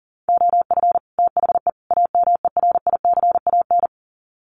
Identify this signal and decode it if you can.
Morse code, heard as a keyed tone, about 40 wpm, 700 Hz. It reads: OF THE AMERICAN
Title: Sets of 3 Words 40wpm